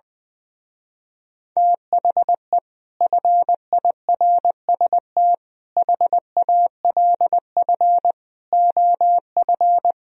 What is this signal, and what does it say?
Morse code, heard as a keyed tone, about 20 wpm, 700 Hz. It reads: THE FIRST HALF OF
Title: Speed-Racing - Sets of 5 Words 20wpm